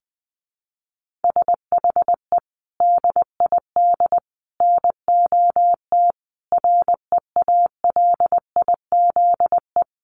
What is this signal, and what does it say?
Morse code, heard as a keyed tone, about 19 wpm, 700 Hz. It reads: SHE DID NOT REALIZE
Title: Sentences from Top 1000 Words 20wpm